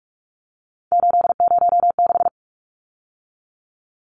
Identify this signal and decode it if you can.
Morse code, heard as a keyed tone, about 44 wpm, 700 Hz. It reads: 806
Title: Numbers - 3-Digit 45wpm